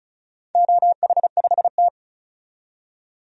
Morse code, heard as a keyed tone, about 36 wpm, 700 Hz. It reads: OH5T
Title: DXCC-100 Call Signs - 4 Characters - 35wpm